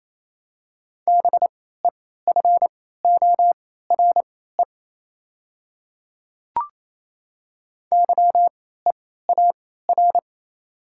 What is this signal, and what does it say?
Morse code, heard as a keyed tone, about 28 wpm, spaced as for 16 wpm, 700 Hz. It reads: BEFORE YEAR
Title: 3x Character Spacing - Top 200 Words - 28wpm